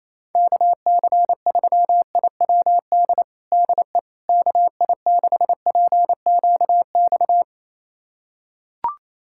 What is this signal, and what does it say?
Morse code, heard as a keyed tone, about 28 wpm, 700 Hz. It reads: KC3IWD DE KI6PQX
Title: USA Callsign DE Callsign - 6 Characters - 28wpm